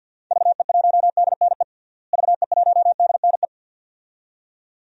Morse code, heard as a keyed tone, about 52 wpm, 700 Hz. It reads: VE1DNE VE1DNE
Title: DXCC-100 Call Signs - 6 Characters - Repeated 2x - 50wpm